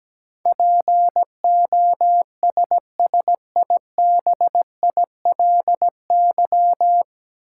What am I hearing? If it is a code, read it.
Morse code, heard as a keyed tone, about 17 wpm, 700 Hz. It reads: POSSIBILY
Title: Sentences from Top 1300 Words - New Word Drill - 17wpm